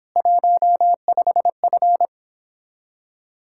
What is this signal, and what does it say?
Morse code, heard as a keyed tone, about 31 wpm, 700 Hz. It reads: 15F